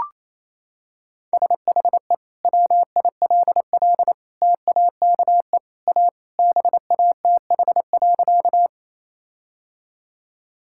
Morse code, heard as a keyed tone, about 28 wpm, 700 Hz. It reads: SHE WILL TAKE A BATH.
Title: Sentences from Top 1600 Words 28wpm